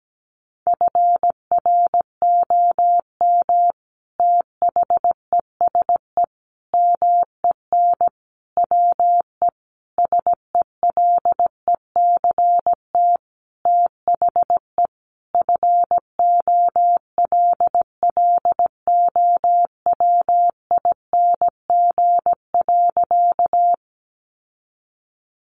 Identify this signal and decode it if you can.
Morse code, heard as a keyed tone, about 17 wpm, 700 Hz. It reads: FROM THESE MEN WE SELECT THE FOLLOWING.
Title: Sentences from Top 1300 Words 17wpm